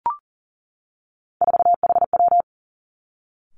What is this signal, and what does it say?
Morse code, heard as a keyed tone, about 40 wpm, 700 Hz. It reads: 4HW
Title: Three Challenging Letter-Number Combinations 40wpm